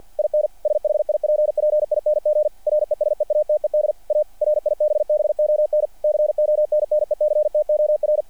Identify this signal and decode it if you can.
Morse code, heard as a keyed tone, about 48 wpm, 600 Hz. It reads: IN SHIPPING RESEATED A RIBBON CONNECTOR
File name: archived / CRSnet